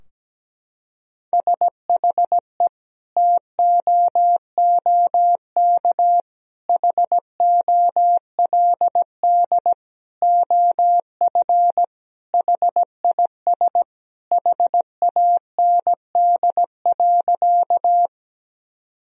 Morse code, heard as a keyed tone, about 17 wpm, 700 Hz. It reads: SHE TOOK HOLD OF HIS HAND.
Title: Sentences from Top 1000 Words - Review Part 2 - 17wpm